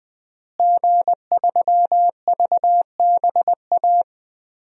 Morse code, heard as a keyed tone, about 20 wpm, 700 Hz. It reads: G3VBA
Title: DXCC-100 Call Signs - 5 Characters - Repeated 3x - 20wpm